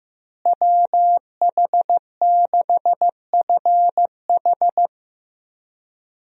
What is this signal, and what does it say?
Morse code, heard as a keyed tone, about 15 wpm, 700 Hz. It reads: WH6FH
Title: USA Call Signs - 5 Characters - Repeated 3x - 15wpm